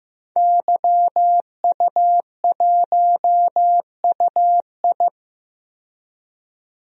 Morse code, heard as a keyed tone, about 15 wpm, 700 Hz. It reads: YU1UI
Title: DXCC-100 Call Signs - 5 Characters - 15wpm